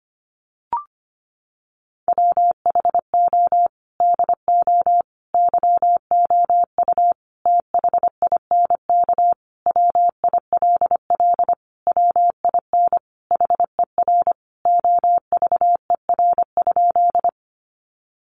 Morse code, heard as a keyed tone, about 25 wpm, 700 Hz. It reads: WHO DO YOU THINK WILL WIN HER OVER?
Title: Sentences from Top 1500 Words 25wpm